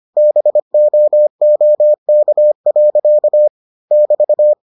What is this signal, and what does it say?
Morse code, heard as a keyed tone, about 25 wpm, 600 Hz. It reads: BOOK. <BT>